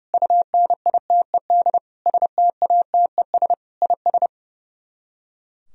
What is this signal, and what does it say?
Morse code, heard as a keyed tone, about 34 wpm, 700 Hz. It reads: UNITED STATES IS
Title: Speed-Racing - Sets of 4 Words 30wpm